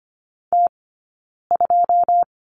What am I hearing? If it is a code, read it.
Morse code, heard as a keyed tone, about 25 wpm, 700 Hz. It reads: T 2
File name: Lesson 17d - Letter u - Two Characters 25wpm